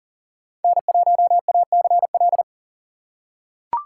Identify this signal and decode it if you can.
Morse code, heard as a keyed tone, about 40 wpm, 700 Hz. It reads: N1ACL
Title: USA Call Signs - 5 Characters - 3x Character Spacing - 40wpm